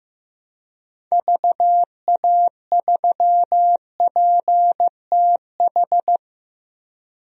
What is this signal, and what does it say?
Morse code, heard as a keyed tone, about 15 wpm, 700 Hz. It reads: VA3PTH
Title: DXCC-100 Call Signs - 6 Characters - Repeated 2x - 15wpm